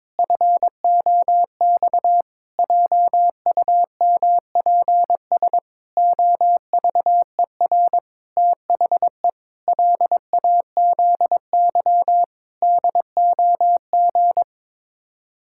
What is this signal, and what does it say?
Morse code, heard as a keyed tone, about 22 wpm, 700 Hz. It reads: FOX JUMPS OVER THE LAZY DOG
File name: practice / 22wpm / Sets of 8 Words - 22wpm